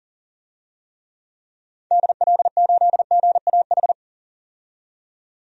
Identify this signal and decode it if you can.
Morse code, heard as a keyed tone, about 37 wpm, 700 Hz. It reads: DL8GAH
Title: DXCC-100 Call Signs - 6 Characters - 40wpm